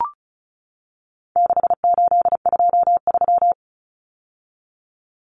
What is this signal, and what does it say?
Morse code, heard as a keyed tone, about 34 wpm, 700 Hz. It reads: 6823
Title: Numbers - 4-Digit 35wpm